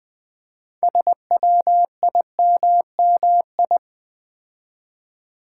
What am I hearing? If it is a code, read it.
Morse code, heard as a keyed tone, about 20 wpm, 700 Hz. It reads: SWIMMI
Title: Build A Word - 2 Letters At A Time - 8-Letter Words - 20wpm